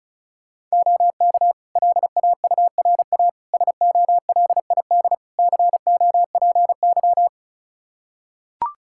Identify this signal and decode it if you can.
Morse code, heard as a keyed tone, about 36 wpm, 700 Hz. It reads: OK LAURA SOLID COPY
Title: K1USN SST QSO Practice 35wpm